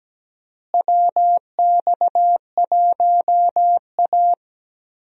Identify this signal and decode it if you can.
Morse code, heard as a keyed tone, about 17 wpm, 700 Hz. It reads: WX1A
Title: USA Call Signs - 4 Characters - Repeated 2x - 17wpm